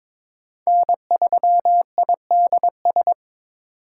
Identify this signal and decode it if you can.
Morse code, heard as a keyed tone, about 22 wpm, 700 Hz. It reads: N3IDS
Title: USA Call Signs - 2x Character Spacing - 22wpm